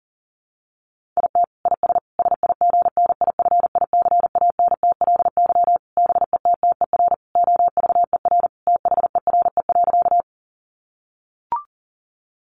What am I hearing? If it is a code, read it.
Morse code, heard as a keyed tone, about 40 wpm, 700 Hz. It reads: IT IS SIGNIFICANTLY BETTER OVER THERE.
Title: Sentences from Top 1500 Words 40wpm